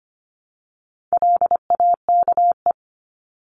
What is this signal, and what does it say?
Morse code, heard as a keyed tone, about 25 wpm, 700 Hz. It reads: LAKE
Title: Top 1001-1250 Words 25wpm